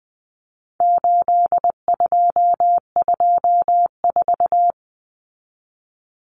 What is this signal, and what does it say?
Morse code, heard as a keyed tone, about 20 wpm, 700 Hz. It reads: 8224
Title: Common Cut Numbers - 4-Digit 20wpm